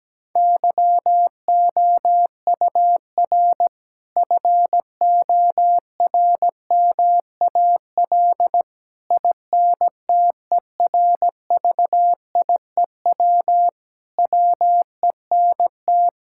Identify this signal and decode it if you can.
Morse code, heard as a keyed tone, about 17 wpm, 700 Hz. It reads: YOUR FORMAL INTERVIEW WENT
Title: Sentences from Top 1600 Words 17wpm